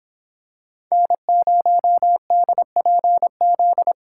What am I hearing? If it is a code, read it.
Morse code, heard as a keyed tone, about 27 wpm, 700 Hz. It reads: N0DPZ
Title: Speed Racing - USA 1x3 Call Signs 22wpm